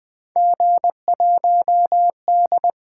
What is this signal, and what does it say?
Morse code, heard as a keyed tone, about 20 wpm, 700 Hz. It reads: G1D
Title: Three Letter-Number Combinations - 20wpm